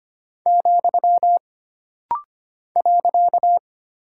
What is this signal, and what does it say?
Morse code, heard as a keyed tone, about 25 wpm, 700 Hz. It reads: , .
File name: Single Punctuation-Prosign - Rapid-Fire - 25wpm